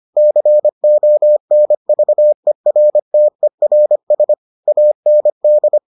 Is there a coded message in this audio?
Yes: Morse code from 0.2 to 5.8 s, heard as a keyed tone, about 25 wpm, 600 Hz. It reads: CONVERTERS AND